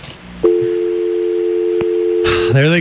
Dial tone (North America), 350 + 440 Hz.